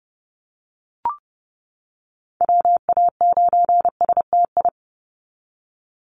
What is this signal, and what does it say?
Morse code, heard as a keyed tone, about 30 wpm, 700 Hz. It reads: WA9STI
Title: NA SOTA Chaser Callsigns 30wpm